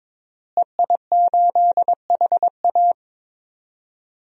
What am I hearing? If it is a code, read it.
Morse code, heard as a keyed tone, about 21 wpm, 700 Hz. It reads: EI8HA
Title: DXCC-100 Call Signs - 5 Characters - Repeated 3x - 22wpm